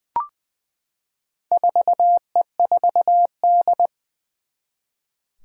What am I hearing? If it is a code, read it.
Morse code, heard as a keyed tone, about 20 wpm, 700 Hz. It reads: 4E4D
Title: Four Challenging Letter-Number Combinations 20wpm